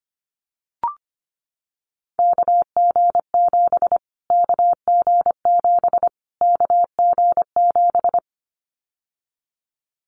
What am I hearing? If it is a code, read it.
Morse code, heard as a keyed tone, about 25 wpm, 700 Hz. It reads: KG7 KG7 KG7
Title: Callsign Prefixes for Top 25 DXCC - Repeated 3x - 25wpm